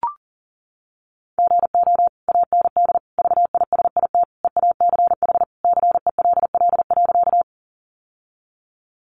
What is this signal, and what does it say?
Morse code, heard as a keyed tone, about 40 wpm, 700 Hz. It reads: GO AND VISIT EACH CELL.
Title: Sentences from Top 1100 Words 40wpm